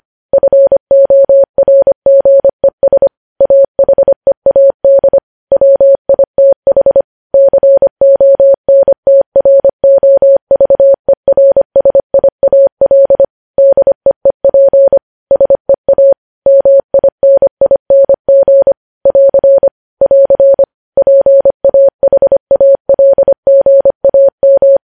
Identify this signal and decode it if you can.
Morse code, heard as a keyed tone, about 25 wpm, 550 Hz. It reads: FORGES AHEAD WITH CONTROVERSIAL DEEP SEA MINING <AR> <AR> PAHALGAM
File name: podcast / News headlines-1x-25x25